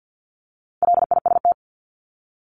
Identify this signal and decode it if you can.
Morse code, heard as a keyed tone, about 50 wpm, 700 Hz. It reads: LIST
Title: Top 500 Words 50wpm